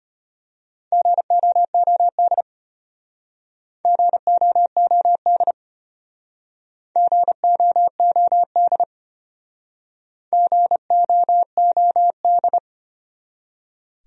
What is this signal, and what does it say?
Morse code, heard as a keyed tone, about 34 wpm, 700 Hz. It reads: GOOD GOOD GOOD GOOD